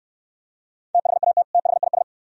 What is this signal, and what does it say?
Morse code, heard as a keyed tone, about 64 wpm, 680 Hz. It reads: THAT THIS